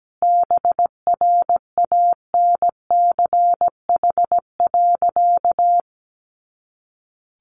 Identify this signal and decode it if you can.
Morse code, heard as a keyed tone, about 17 wpm, 700 Hz. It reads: BRANCH.
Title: A Sound Of Thunder 17wpm